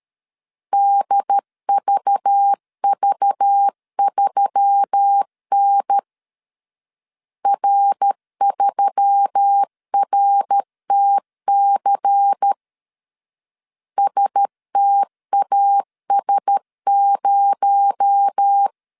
Morse code, heard as a keyed tone, about 13 wpm, 800 Hz. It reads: DVV3N R3RTC STAS0
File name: CW / DET / 1 9 1500C S A 12wpm 3800